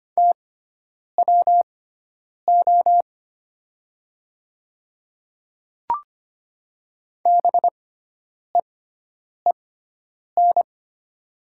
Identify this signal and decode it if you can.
Morse code, heard as a keyed tone, about 25 wpm, spaced as for 9 wpm, 700 Hz. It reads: TWO BEEN
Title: 6x Character Spacing - Top 300 Words - 25wpm